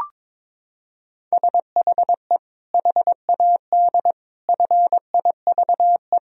Morse code, heard as a keyed tone, about 22 wpm, 700 Hz. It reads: SHE HAD FIVE
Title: Sentences from Top 1000 Words - Review Part 2 - 22wpm